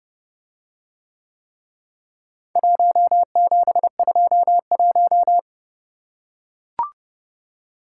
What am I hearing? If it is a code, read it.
Morse code, heard as a keyed tone, about 29 wpm, 700 Hz. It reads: 1721